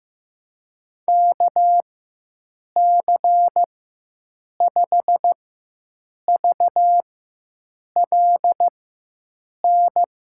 Morse code, heard as a keyed tone, about 15 wpm, spaced as for 7 wpm, 700 Hz. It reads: KC5VLN
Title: USA Call Signs - 6 Characters - 4x Character Spacing - 15wpm